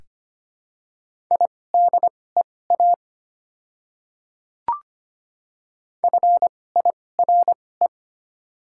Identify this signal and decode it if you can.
Morse code, heard as a keyed tone, about 25 wpm, spaced as for 18 wpm, 700 Hz. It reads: IDEA FIRE
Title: 2x Character Spacing - Top 500 Words - 25wpm